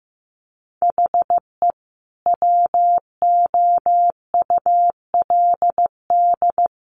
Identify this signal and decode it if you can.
Morse code, heard as a keyed tone, about 15 wpm, 700 Hz. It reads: HE WOULD
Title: Sentences from Top 1500 Words 15wpm